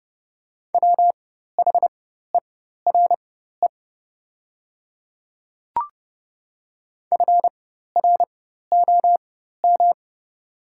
Morse code, heard as a keyed tone, about 30 wpm, spaced as for 14 wpm, 700 Hz. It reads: WHERE FROM
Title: 4x Character Spacing - Top 300 Words - 30wpm